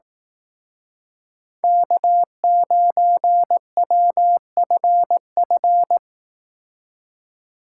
Morse code, heard as a keyed tone, about 18 wpm, 700 Hz. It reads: K9WFF
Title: Speed Racing - USA 1x3 Call Signs 15wpm